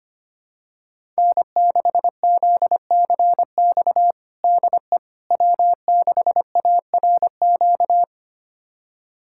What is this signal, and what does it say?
Morse code, heard as a keyed tone, about 25 wpm, 700 Hz. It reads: N6ZCX DE W6ARQ
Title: USA Callsign DE Callsign - 5 Characters - 25wpm